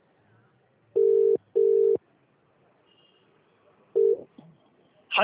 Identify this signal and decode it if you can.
Ringback tone (UK), 400 + 450 Hz.